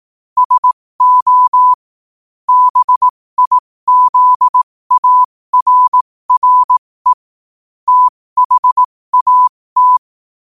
Morse code, heard as a keyed tone, about 18 wpm, spaced as for 15 wpm, 1000 Hz. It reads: SO BIZARRE THAT
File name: qotdfiles / 15wpm